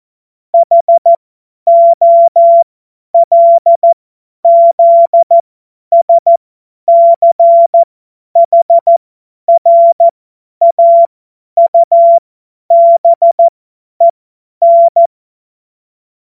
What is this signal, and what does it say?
Morse code, heard as a keyed tone, about 14 wpm, spaced as for 10 wpm, 680 Hz. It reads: HOLZSCHRAUBEN
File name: Morsekurs / Witze / Witz 17.Oktober 2023 50 F70